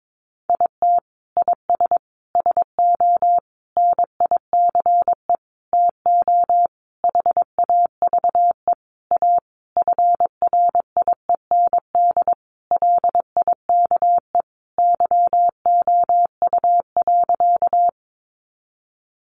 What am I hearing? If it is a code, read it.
Morse code, heard as a keyed tone, about 22 wpm, 700 Hz. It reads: IT IS SO NICE TO HAVE A FRIEND LIKE YOU.
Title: Sentences from Top 1100 Words 22wpm